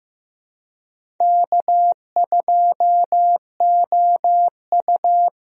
Morse code, heard as a keyed tone, about 15 wpm, 700 Hz. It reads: K2OU
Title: USA Call Signs - 4 Characters - Repeated 3x - 15wpm